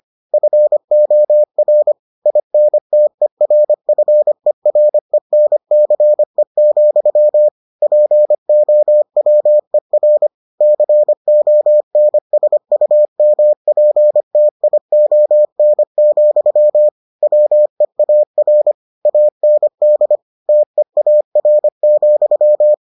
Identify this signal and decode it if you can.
Morse code, heard as a keyed tone, about 25 wpm, 600 Hz. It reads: FOR INTERFERENCE, POWER CONSUMPTION, WEAR AND TEAR,